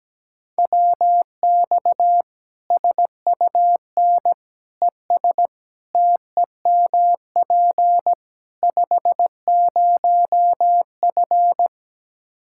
Morse code, heard as a keyed tone, about 17 wpm, 700 Hz. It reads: WX SUN ES TEMP 50F
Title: Practice QSO - Exchange 2 - Info - 17wpm